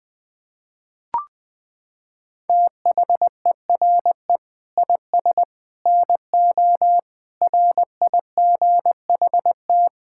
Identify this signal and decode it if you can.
Morse code, heard as a keyed tone, about 21 wpm, 700 Hz. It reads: THERE IS NO RIGHT
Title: Sentences with 8-Word Phrases - 20wpm